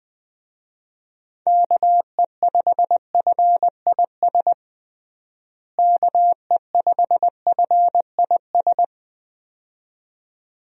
Morse code, heard as a keyed tone, about 20 wpm, 700 Hz. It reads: KE5FIS KE5FIS
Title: USA Call Signs - 6 Characters - Repeated 2x - 20wpm